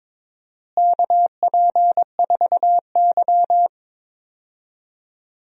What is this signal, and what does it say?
Morse code, heard as a keyed tone, about 22 wpm, 700 Hz. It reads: KP4Y
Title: DXCC-100 Call Signs - 4 Characters - Repeated 2x - 22wpm